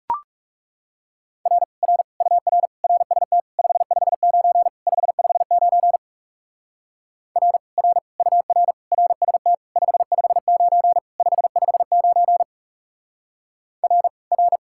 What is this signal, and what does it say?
Morse code, heard as a keyed tone, about 37 wpm, 700 Hz. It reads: R R UR RST 559 559 R R UR RST 559 559 R R
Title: Speed-Racing Common QSO Phrases Encore - 30wpm